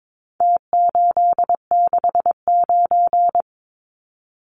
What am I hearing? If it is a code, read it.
Morse code, heard as a keyed tone, about 22 wpm, 700 Hz. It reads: T869
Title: Numbers - Contest Style - 4-Digit 22wpm